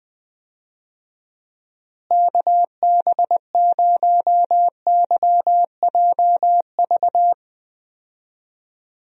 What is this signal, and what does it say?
Morse code, heard as a keyed tone, about 20 wpm, 700 Hz. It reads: KB0YJV
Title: USA Call Signs - 6 Characters - Repeated 2x - 20wpm